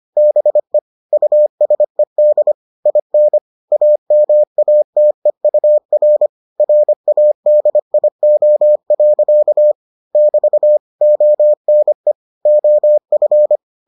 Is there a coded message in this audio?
Yes: Morse code from 0.2 to 13.6 s, heard as a keyed tone, about 25 wpm, 600 Hz. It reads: BE USED IN AMATEUR RADIO. <BT> ONE OF